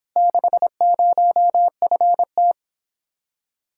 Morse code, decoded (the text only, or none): 60FT